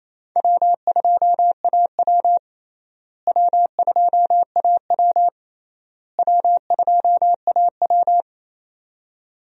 Morse code, heard as a keyed tone, about 28 wpm, 700 Hz. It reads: W2AW W2AW W2AW
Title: USA Call Signs - 4 Characters - Repeated 3x - 28wpm